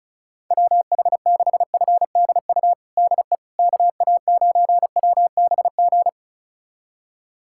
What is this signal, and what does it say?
Morse code, heard as a keyed tone, about 36 wpm, 700 Hz. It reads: WH6FDU DE KA9WBG
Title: USA Callsign DE Callsign - 6 Characters - 35wpm